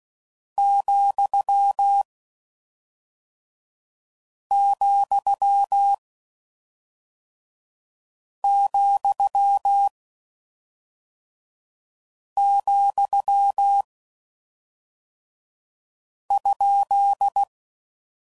Morse code, heard as a keyed tone, about 16 wpm, 800 Hz. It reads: , , , , ?
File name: Mixed / Misc / Punct